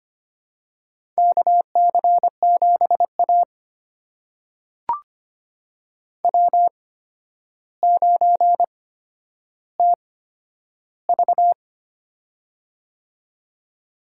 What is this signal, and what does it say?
Morse code, heard as a keyed tone, about 25 wpm, spaced as for 11 wpm, 700 Hz. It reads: KC7A W9TV
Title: USA Call Signs - 4 Characters - 8x Character Spacing - 25wpm